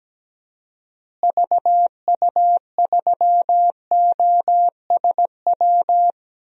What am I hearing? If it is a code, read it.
Morse code, heard as a keyed tone, about 17 wpm, 700 Hz. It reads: VU3OSW
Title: DXCC-100 Call Signs - 6 Characters - Repeated 2x - 17wpm